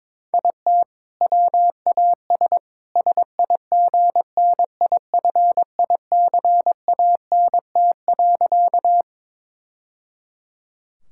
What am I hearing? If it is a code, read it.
Morse code, heard as a keyed tone, about 22 wpm, 700 Hz. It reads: IT WAS SIGNIFICANT.
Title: Sentences from Top 1000 Words - Review Part 6 - 22wpm